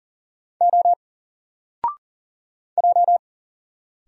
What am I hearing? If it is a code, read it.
Morse code, heard as a keyed tone, about 40 wpm, 700 Hz. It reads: O J
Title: Single Letter-Number Alt 40wpm